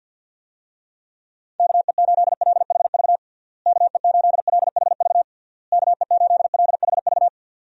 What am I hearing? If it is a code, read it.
Morse code, heard as a keyed tone, about 52 wpm, 700 Hz. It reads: KE8LHV KE8LHV KE8LHV
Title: USA Call Signs - 6 Characters - Repeated 3x - 50wpm